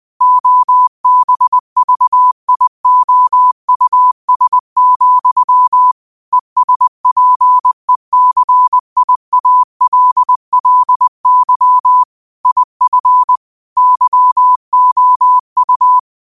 Morse code, heard as a keyed tone, about 20 wpm, 1000 Hz. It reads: OBVIOUS, ESPECIALLY IF YOU